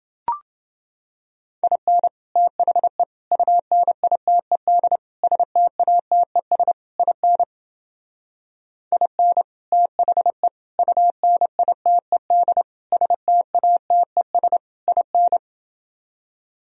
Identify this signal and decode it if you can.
Morse code, heard as a keyed tone, about 28 wpm, 700 Hz. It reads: IN THE UNITED STATES IN IN THE UNITED STATES IN